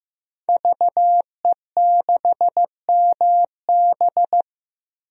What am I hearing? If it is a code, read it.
Morse code, heard as a keyed tone, about 15 wpm, 700 Hz. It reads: VE6MB
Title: DXCC-100 Call Signs - 5 Characters - 15wpm